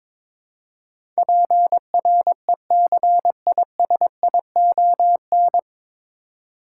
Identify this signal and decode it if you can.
Morse code, heard as a keyed tone, about 22 wpm, 700 Hz. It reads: PRECISION
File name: 9 Letter Words - First 3 Letters Given - 22wpm